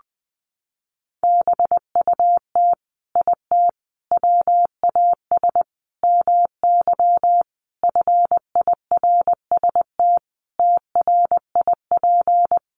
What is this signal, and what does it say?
Morse code, heard as a keyed tone, about 20 wpm, 700 Hz. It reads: BUT IT WAS MY FIRST TRIP
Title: Sentences from Top 1300 Words 20wpm